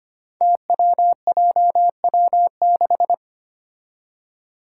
Morse code, heard as a keyed tone, about 25 wpm, 700 Hz. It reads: TWJW6
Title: Five Challenging Letter-Number Combinations 25wpm